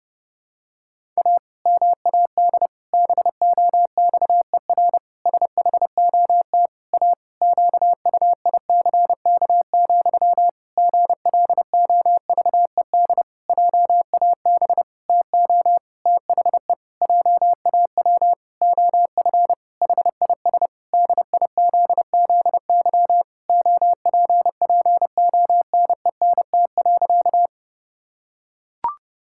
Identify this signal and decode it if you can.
Morse code, heard as a keyed tone, about 30 wpm, 700 Hz. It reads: A MAD BOXER SHOT A QUICK, GLOVED JAB TO THE JAW OF HIS DIZZY OPPONENT.
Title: Pangram Challenge v2 - 30wpm